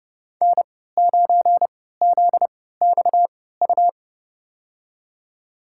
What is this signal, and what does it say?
Morse code, heard as a keyed tone, about 31 wpm, spaced as for 17 wpm, 700 Hz. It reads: N9ZXU
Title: USA Call Signs - 3x Character Spacing - 30wpm